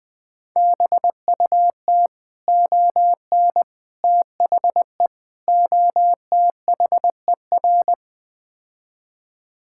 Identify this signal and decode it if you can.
Morse code, heard as a keyed tone, about 21 wpm, 700 Hz. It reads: BUT ON THE OTHER
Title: Build A Phrase - 5-Word Phrases - 20wpm